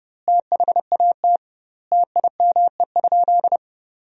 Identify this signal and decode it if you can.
Morse code, heard as a keyed tone, about 30 wpm, 700 Hz. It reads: THAT TIME?
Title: 2x Word-Spacing - Sentences from Top 200 Words - 30wpm